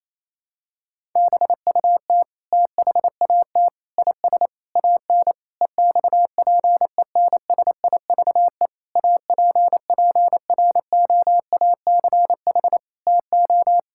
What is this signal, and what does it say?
Morse code, heard as a keyed tone, about 28 wpm, 700 Hz. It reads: BUT THAT IS AN EXPENSIVE APPROACH TO
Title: Sentences from Top 1000 Words - Review Part 6 - 28wpm